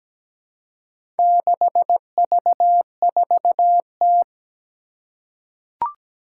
Morse code, heard as a keyed tone, about 17 wpm, 700 Hz. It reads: 6V4T
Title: Four Challenging Letter-Number Combinations 17wpm